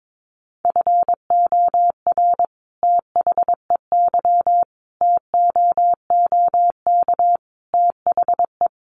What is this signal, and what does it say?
Morse code, heard as a keyed tone, about 22 wpm, 700 Hz. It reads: FOR THEY TOOK THE
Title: Sentences from Top 1400 Words 22wpm